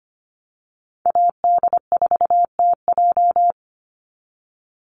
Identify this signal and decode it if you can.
Morse code, heard as a keyed tone, about 25 wpm, 700 Hz. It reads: AD4TJ
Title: Call Signs - 25wpm